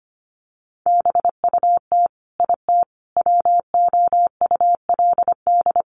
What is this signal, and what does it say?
Morse code, heard as a keyed tone, about 25 wpm, 700 Hz. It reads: BUT IT WOULD
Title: Sentences from Top 1400 Words 25wpm